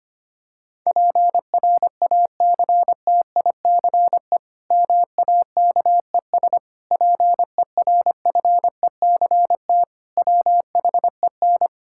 Morse code, heard as a keyed tone, about 25 wpm, 700 Hz. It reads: PRACTICE MAKES PERFECT WHEN
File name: practice / 25wpm / Difficult Sentences with Hints - 25wpm